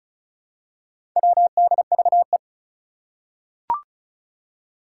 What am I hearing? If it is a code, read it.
Morse code, heard as a keyed tone, about 36 wpm, 700 Hz. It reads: WDVE
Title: Four Challenging Letter-Number Combinations 35wpm